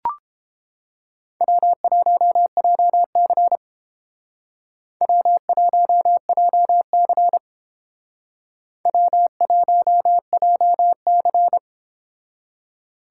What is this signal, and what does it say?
Morse code, heard as a keyed tone, about 29 wpm, 700 Hz. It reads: W1JC W1JC W1JC